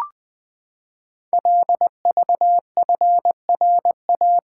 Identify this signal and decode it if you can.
Morse code, heard as a keyed tone, about 20 wpm, 700 Hz. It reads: LVFRA
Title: Code Groups - 5 Letters - 20wpm